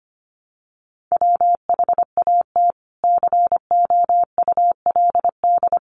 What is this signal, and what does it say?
Morse code, heard as a keyed tone, about 25 wpm, 700 Hz. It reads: WHAT COULD
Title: Sentences from Top 600 Words 25wpm